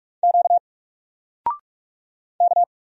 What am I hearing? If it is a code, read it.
Morse code, heard as a keyed tone, about 44 wpm, 700 Hz. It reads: Q K
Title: Single Character (Letter:Number:Punctuation:Prosign) 45wpm